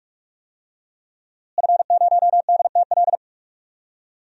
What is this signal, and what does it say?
Morse code, heard as a keyed tone, about 46 wpm, 700 Hz. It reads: F0DTL
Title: DXCC-100 Call Signs - 5 Characters - Repeated 3x - 45wpm